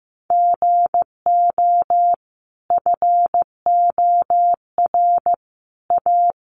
Morse code, heard as a keyed tone, about 15 wpm, 700 Hz. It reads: GO FOR A
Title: Sentences from Top 1300 Words 15wpm